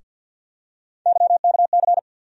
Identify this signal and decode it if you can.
Morse code, heard as a keyed tone, about 51 wpm, 700 Hz. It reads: YKC
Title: USA Call Sign Postfixes - 3 Letters 50wpm